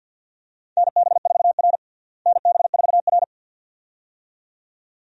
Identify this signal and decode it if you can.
Morse code, heard as a keyed tone, about 47 wpm, 700 Hz. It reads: NB4R NB4R
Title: USA Call Signs - 4 Characters - Repeated 2x - 50wpm